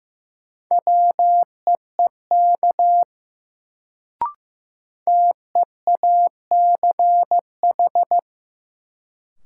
Morse code, heard as a keyed tone, about 16 wpm, 700 Hz. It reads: WEEK TEACH
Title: Sentences from Top 500 Words - New Word Drill - 15wpm